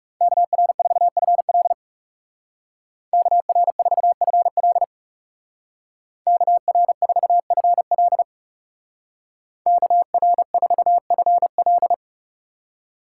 Morse code, heard as a keyed tone, about 35 wpm, 700 Hz. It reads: KR4FL KR4FL KR4FL KR4FL
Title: Speed Racing - USA 2x2 Call Signs 30wpm